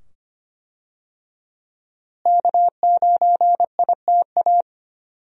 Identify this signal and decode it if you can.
Morse code, heard as a keyed tone, about 25 wpm, 700 Hz. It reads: K9ITA
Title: USA Call Signs - 5 Characters - Repeated 2x - 25wpm